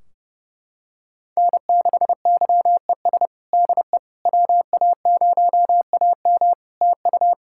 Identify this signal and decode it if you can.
Morse code, heard as a keyed tone, about 32 wpm, 700 Hz. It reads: N6YES DE WA0AM TU
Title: Practice QSO - Exchange 4 - Ending - 30wpm